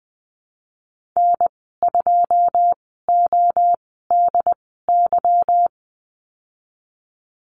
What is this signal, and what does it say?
Morse code, heard as a keyed tone, about 19 wpm, spaced as for 14 wpm, 700 Hz. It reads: N2ODY
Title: 147 Lesson 24g - Letter y - Call Signs 20wpm